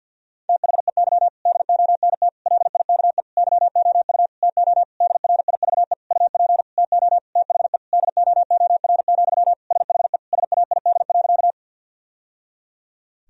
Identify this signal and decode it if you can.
Morse code, heard as a keyed tone, about 51 wpm, 700 Hz. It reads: THEY DONT LIKE YOU TO DRIVE UP TO THE DOOR, SHE SAID.
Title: Sentences from Top 1000 Words - Review Part 2 - 50wpm